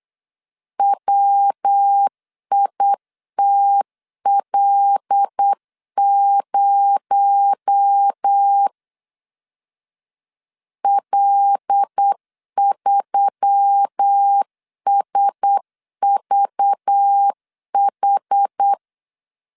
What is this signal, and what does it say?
Morse code, heard as a keyed tone, about 8 wpm, 800 Hz. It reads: WITL0 L3SVH
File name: LEZIONI / 7 GF9Z / 7 G 8wpm 3118